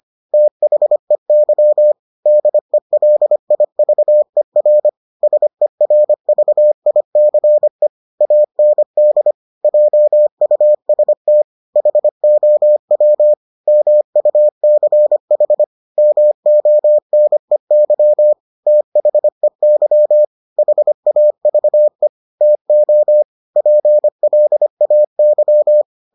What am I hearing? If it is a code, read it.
Morse code, heard as a keyed tone, about 25 wpm, 600 Hz. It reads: THEY DELIVER SERVICE AND JUST HOW MUCH MONEY THEY HAVE TO PLAY